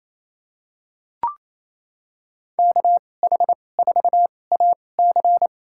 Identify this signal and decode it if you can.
Morse code, heard as a keyed tone, about 28 wpm, spaced as for 20 wpm, 700 Hz. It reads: KH4AC
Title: USA Call Signs - 2x Character Spacing - 28wpm